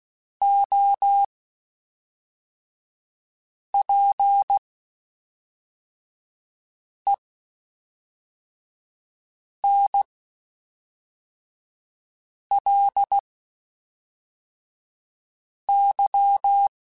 Morse code, heard as a keyed tone, about 16 wpm, spaced as for 3 wpm, 800 Hz. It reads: OPENLY
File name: Quotes Spaced / Quotes Spaced-mp3s / Advice sp @5